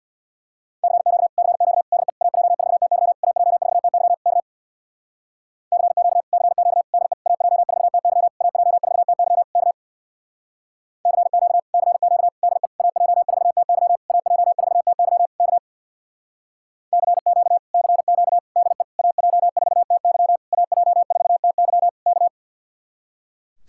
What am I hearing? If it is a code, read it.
Morse code, heard as a keyed tone, about 60 wpm, 700 Hz. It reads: CQ CQ DE AJ4TY AJ4TY K CQ CQ DE AJ4TY AJ4TY K CQ CQ DE AJ4TY AJ4TY K CQ CQ DE AJ4TY AJ4TY K
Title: Speed-Racing Common QSO Phrases Encore - 50wpm